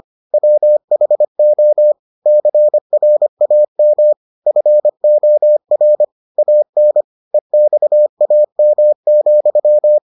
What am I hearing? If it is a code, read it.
Morse code, heard as a keyed tone, about 25 wpm, 600 Hz. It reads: WHO CRAM FOR AN EXAM,